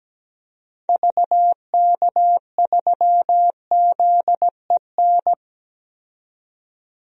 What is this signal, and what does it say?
Morse code, heard as a keyed tone, about 17 wpm, 700 Hz. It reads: VK3ZEN
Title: DXCC-100 Call Signs - 6 Characters - 17wpm